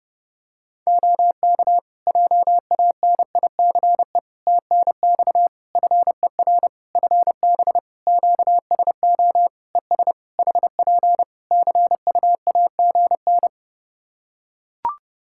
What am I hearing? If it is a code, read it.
Morse code, heard as a keyed tone, about 30 wpm, 700 Hz. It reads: OK JANICE TNX FER FB QSO ES HP CUAGN
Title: K1USN SST QSO Practice 30wpm